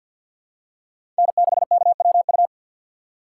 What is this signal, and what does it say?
Morse code, heard as a keyed tone, about 49 wpm, 700 Hz. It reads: N6KWU
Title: USA Call Signs - 5 Characters - 4x Character Spacing - 50wpm